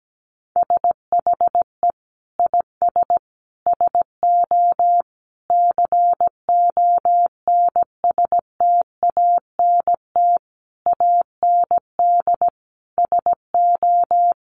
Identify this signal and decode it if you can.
Morse code, heard as a keyed tone, about 17 wpm, 700 Hz. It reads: SHE IS SO CONSTANT AND SO
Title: Sentences from Top 1500 Words 17wpm